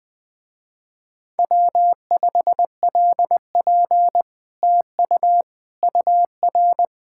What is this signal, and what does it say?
Morse code, heard as a keyed tone, about 20 wpm, 700 Hz. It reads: W5LP TU UR
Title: Speed-Racing Common QSO Phrases Encore - 17wpm